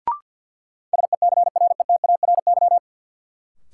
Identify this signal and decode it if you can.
Morse code, heard as a keyed tone, about 53 wpm, 700 Hz. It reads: SECRETARY
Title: Sentences from Top 1300 Words - New Word Drill - 50wpm